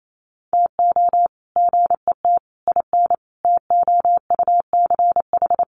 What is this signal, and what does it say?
Morse code, heard as a keyed tone, about 28 wpm, 700 Hz. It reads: TO GET IN TOUCH
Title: Sets of 5 Words Encore2 28wpm